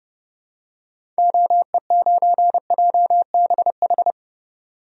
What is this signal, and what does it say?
Morse code, heard as a keyed tone, about 30 wpm, 700 Hz. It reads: OE9JBH
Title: DXCC-100 Call Signs - 6 Characters - 30wpm